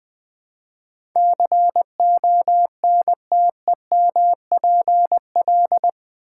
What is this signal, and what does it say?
Morse code, heard as a keyed tone, about 20 wpm, 700 Hz. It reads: CONTEMPL
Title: Build A Word - 4 Letters At A Time - 11-Letter Words - 20wpm